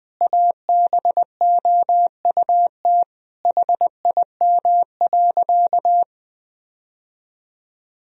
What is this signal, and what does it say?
Morse code, heard as a keyed tone, about 21 wpm, 700 Hz. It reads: ABOUT HIM.
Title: Sentences from Top 1000 Words - Review Part 4 - 20wpm